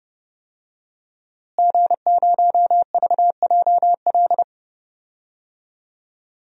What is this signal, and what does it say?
Morse code, heard as a keyed tone, about 30 wpm, 700 Hz. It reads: G0VJL
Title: DXCC-100 Call Signs - 5 Characters - Repeated 3x - 30wpm